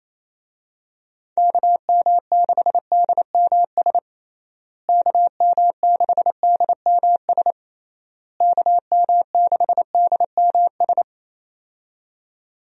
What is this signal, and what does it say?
Morse code, heard as a keyed tone, about 28 wpm, 700 Hz. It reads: KM6DMS KM6DMS KM6DMS
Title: USA Call Signs - 6 Characters - Repeated 3x - 28wpm